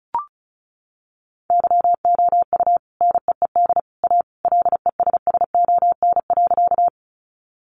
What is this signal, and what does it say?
Morse code, heard as a keyed tone, about 34 wpm, 700 Hz. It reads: YOU NEED A LESSON.
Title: Sentences from Top 1200 Words 35wpm